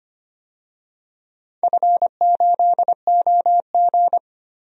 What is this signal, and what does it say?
Morse code, heard as a keyed tone, about 25 wpm, 700 Hz. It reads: F8OG
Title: DXCC-100 Call Signs - 4 Characters - 25wpm